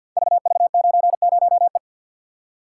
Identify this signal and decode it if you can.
Morse code, heard as a keyed tone, about 50 wpm, 700 Hz. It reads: VV90E
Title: Code Groups - 5 Letter:Numbers - 50wpm